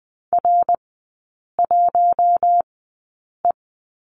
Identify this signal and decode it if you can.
Morse code, heard as a keyed tone, about 20 wpm, spaced as for 8 wpm, 700 Hz. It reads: R1E
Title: 060 Lesson 11e - Letter h - Three Characters 20wpm